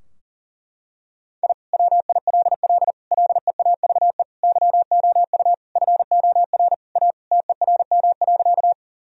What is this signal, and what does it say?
Morse code, heard as a keyed tone, about 37 wpm, 700 Hz. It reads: I WILL LEAVE YOU FOR A TERM.
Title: Sentences from Top 1000 Words - Review Part 4 - 40wpm